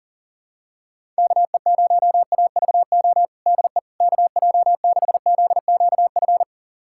Morse code, heard as a keyed tone, about 40 wpm, 700 Hz. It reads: KE0AVO DE KJ6ZQF
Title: USA Callsign DE Callsign 40wpm